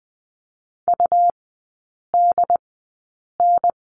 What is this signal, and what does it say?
Morse code, heard as a keyed tone, about 20 wpm, spaced as for 8 wpm, 700 Hz. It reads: UDN
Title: 121 Lesson 20e - Number 3 - Three Characters 20wpm